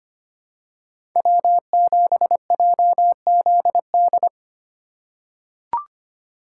Morse code, heard as a keyed tone, about 25 wpm, 700 Hz. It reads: W7JZD
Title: USA Call Signs - 5 Characters - 3x Character Spacing - 25wpm